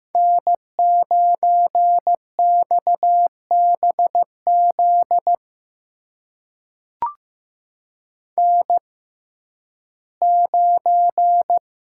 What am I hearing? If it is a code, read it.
Morse code, heard as a keyed tone, about 15 wpm, spaced as for 11 wpm, 700 Hz. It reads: N9XBZ N9
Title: USA Call Signs - 5 Characters - 6x Character Spacing - 15wpm